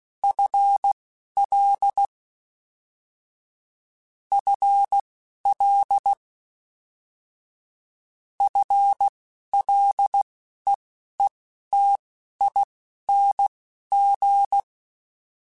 Morse code, heard as a keyed tone, about 16 wpm, spaced as for 11 wpm, 800 Hz. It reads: FL FL FLEETING
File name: Combinations / Combinations-mp3s / Combinations@10